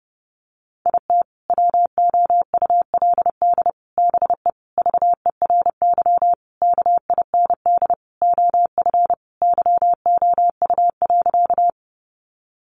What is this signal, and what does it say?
Morse code, heard as a keyed tone, about 30 wpm, 700 Hz. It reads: IT WOULD BE VERY KIND OF YOU.
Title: Sentences from Top 200 Words 30wpm